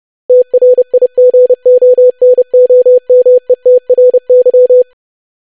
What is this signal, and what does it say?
Morse code, heard as a keyed tone, about 30 wpm, 500 Hz. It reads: TRIGONOMETRY
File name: lesson / web / longwords-25wpm